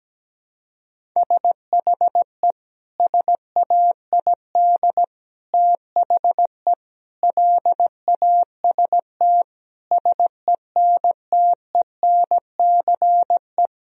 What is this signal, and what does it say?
Morse code, heard as a keyed tone, about 17 wpm, 700 Hz. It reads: SHE SAID THE LAST SENTENCE
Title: Sentences from Top 1000 Words - Review Part 1 - 17wpm